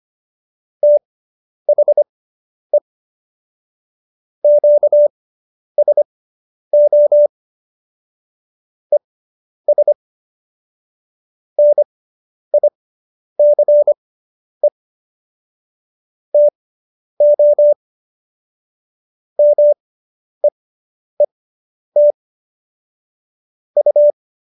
Morse code, heard as a keyed tone, about 25 wpm, spaced as for 10 wpm, 590 Hz. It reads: THE QSO ES NICE TO MEET U